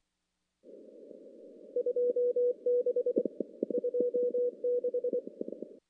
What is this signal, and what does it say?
Morse code, heard as a keyed tone, about 25 wpm, 490 Hz. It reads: 2B 26
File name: contests / CW-SS-k5tr- -0000